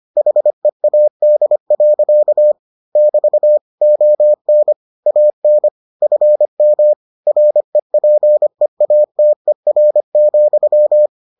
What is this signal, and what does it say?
Morse code, heard as a keyed tone, about 25 wpm, 600 Hz. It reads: HEAD. <BT> ON AN FM REPEATER,